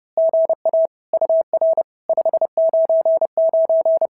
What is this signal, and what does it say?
Morse code, heard as a keyed tone, about 30 wpm, 650 Hz. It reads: GA UR 599
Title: SOTA QSO Practice 30wpm